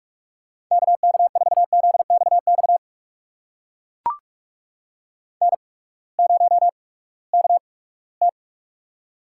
Morse code, heard as a keyed tone, about 44 wpm, spaced as for 22 wpm, 700 Hz. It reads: KK4ZXX N0KT
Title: USA Call Signs - 8x Character Spacing - 45wpm